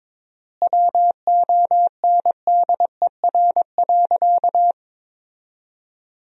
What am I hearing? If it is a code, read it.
Morse code, heard as a keyed tone, about 22 wpm, 700 Hz. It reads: WONDER.
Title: 4x Word-Spacing - Sentences from Top 500 Words - 22wpm